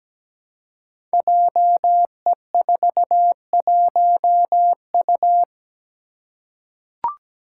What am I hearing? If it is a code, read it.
Morse code, heard as a keyed tone, about 17 wpm, 700 Hz. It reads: JE41U
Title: Five Challenging Letter-Number Combinations 17wpm